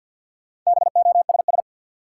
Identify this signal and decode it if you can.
Morse code, heard as a keyed tone, about 50 wpm, 700 Hz. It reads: BOSS
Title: Sentences from Top 700 Words - New Word Drill - 50wpm